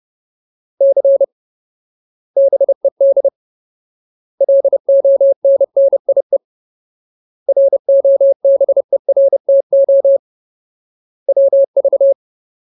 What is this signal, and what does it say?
Morse code, heard as a keyed tone, about 30 wpm, 550 Hz. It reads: C BED LONNIE ROBERTO WV
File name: Ultimate Word List / 30WPM Ultimate